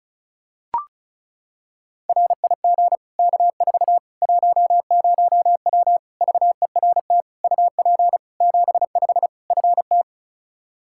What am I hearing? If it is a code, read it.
Morse code, heard as a keyed tone, about 36 wpm, 700 Hz. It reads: RIG K4 10W VERT UP 75 FT
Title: Practice QSO - Exchange 2 - Info - 35wpm